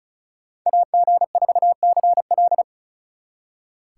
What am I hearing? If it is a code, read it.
Morse code, heard as a keyed tone, about 46 wpm, 700 Hz. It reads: AG4CL